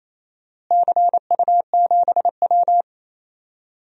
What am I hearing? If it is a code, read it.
Morse code, heard as a keyed tone, about 28 wpm, 700 Hz. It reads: CU7W
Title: DXCC-100 Call Signs - 4 Characters - Repeated 3x - 28wpm